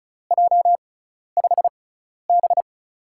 Morse code, heard as a keyed tone, about 36 wpm, spaced as for 12 wpm, 700 Hz. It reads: J5B
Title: USA Call Signs - 6x Character Spacing - 35wpm